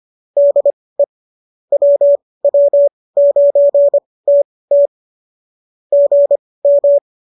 Morse code, heard as a keyed tone, about 25 wpm, spaced as for 18 wpm, 570 Hz. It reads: DE WW9TT GM